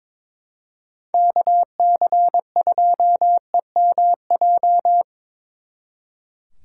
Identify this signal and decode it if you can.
Morse code, heard as a keyed tone, about 25 wpm, 700 Hz. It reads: KC2EMJ
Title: Speed Racing - USA 2x3 Call Signs 22wpm